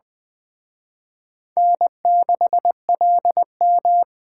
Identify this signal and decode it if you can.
Morse code, heard as a keyed tone, about 20 wpm, 700 Hz. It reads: N6LM